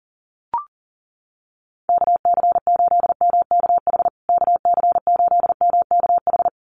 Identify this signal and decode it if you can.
Morse code, heard as a keyed tone, about 40 wpm, 700 Hz. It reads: KC8MKH KC8MKH
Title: USA Call Signs Repeated 2x 40wpm